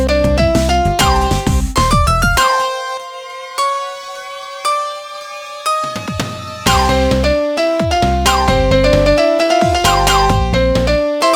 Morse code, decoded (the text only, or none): none